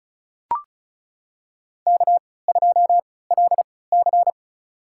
Morse code, heard as a keyed tone, about 36 wpm, spaced as for 20 wpm, 700 Hz. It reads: K2LC
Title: USA Call Signs - 4 Characters - 3x Character Spacing - 35wpm